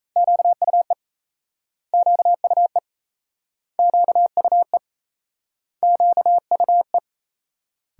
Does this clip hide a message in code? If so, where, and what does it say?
Yes: Morse code from 0.2 to 7.0 s, heard as a keyed tone, about 38 wpm, 700 Hz. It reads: QUE QUE QUE QUE